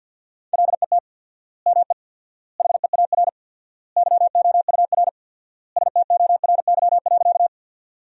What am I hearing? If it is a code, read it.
Morse code, heard as a keyed tone, about 53 wpm, 700 Hz. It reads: LET ME HEAR YOUR STORY.
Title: 4x Word-Spacing - Sentences from Top 300 Words - 50wpm